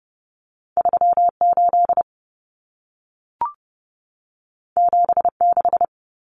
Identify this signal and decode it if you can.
Morse code, heard as a keyed tone, about 30 wpm, 700 Hz. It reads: 38 76
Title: Common Cut Numbers - 2-Digit 30wpm